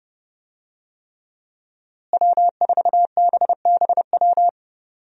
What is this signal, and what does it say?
Morse code, heard as a keyed tone, about 30 wpm, 700 Hz. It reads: W4BBW
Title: USA Call Signs - 5 Characters - Repeated 3x - 30wpm